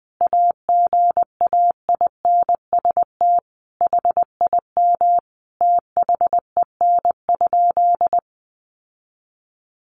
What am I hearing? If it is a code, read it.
Morse code, heard as a keyed tone, about 20 wpm, 700 Hz. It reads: AGAINST HIM THEN?
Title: Sentences from Top 1300 Words 20wpm